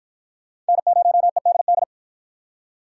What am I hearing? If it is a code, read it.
Morse code, heard as a keyed tone, about 46 wpm, 700 Hz. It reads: N0EDD